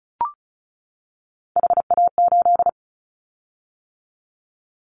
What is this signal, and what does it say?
Morse code, heard as a keyed tone, about 36 wpm, 690 Hz. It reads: HA8
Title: Callsign Prefixes for Top 50 DXCC - 35wpm